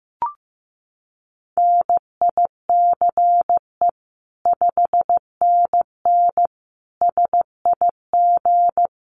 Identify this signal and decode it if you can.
Morse code, heard as a keyed tone, about 15 wpm, 700 Hz. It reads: NICE 5NN SIG
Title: Common QSO Phrases 15wpm